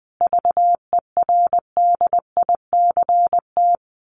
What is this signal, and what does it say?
Morse code, heard as a keyed tone, about 20 wpm, 700 Hz. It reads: VERDICT
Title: A Sound Of Thunder 20wpm